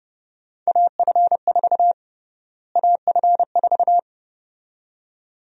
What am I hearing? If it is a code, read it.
Morse code, heard as a keyed tone, about 30 wpm, 700 Hz. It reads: AF4 AF4
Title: USA Call Sign Prefixes - Repeated 2x - 30wpm